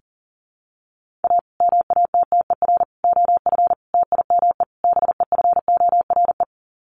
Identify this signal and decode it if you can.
Morse code, heard as a keyed tone, about 40 wpm, 700 Hz. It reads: A MATTER OF TIME BEFORE
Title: Sets of 5 Words Encore 40wpm